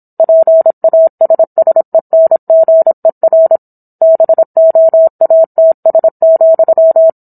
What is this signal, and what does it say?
Morse code, heard as a keyed tone, about 27 wpm, 650 Hz. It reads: PASSENGER BOATS,